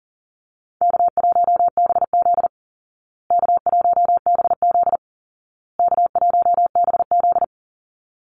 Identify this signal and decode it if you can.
Morse code, heard as a keyed tone, about 40 wpm, 700 Hz. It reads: K1BZ K1BZ K1BZ
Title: Call Signs Repeated 2x - 40wpm